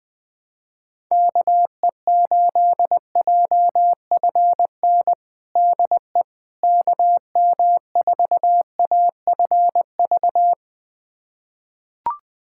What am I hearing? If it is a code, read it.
Morse code, heard as a keyed tone, about 21 wpm, 700 Hz. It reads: KE8JFN DE KM4AFV
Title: USA Callsign DE Callsign - 6 Characters - 20wpm